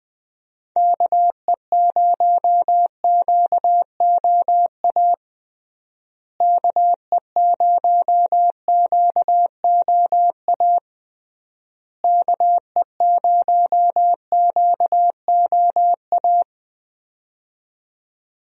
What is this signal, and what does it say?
Morse code, heard as a keyed tone, about 20 wpm, 700 Hz. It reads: KE0QOA KE0QOA KE0QOA
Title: USA Call Signs - 6 Characters - Repeated 3x - 20wpm